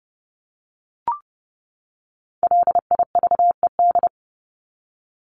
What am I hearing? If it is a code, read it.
Morse code, heard as a keyed tone, about 30 wpm, 700 Hz. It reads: LIVED
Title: ED Suffix - Top 100 Words with Base Words 30wpm